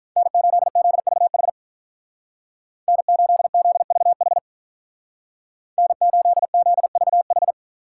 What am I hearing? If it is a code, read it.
Morse code, heard as a keyed tone, about 46 wpm, 700 Hz. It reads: N8ZVH N8ZVH N8ZVH